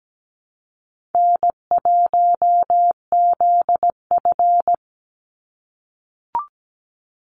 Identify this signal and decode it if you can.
Morse code, heard as a keyed tone, about 17 wpm, 700 Hz. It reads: N1ZF
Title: NA SOTA Activator Callsigns 17wpm